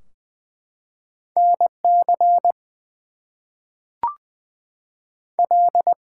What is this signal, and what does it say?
Morse code, heard as a keyed tone, about 20 wpm, 700 Hz. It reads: NC L
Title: US States and Canadian Provinces 20wpm